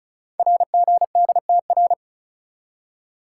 Morse code, heard as a keyed tone, about 36 wpm, 690 Hz. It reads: RGDTR
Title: Code Groups - 5 Letters - 35wpm